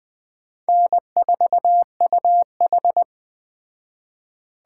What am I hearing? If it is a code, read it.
Morse code, heard as a keyed tone, about 20 wpm, 700 Hz. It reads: N4UH